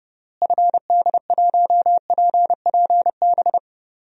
Morse code, heard as a keyed tone, about 30 wpm, 700 Hz. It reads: FD1PPB
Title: DXCC-100 Call Signs - 6 Characters - Repeated 3x - 30wpm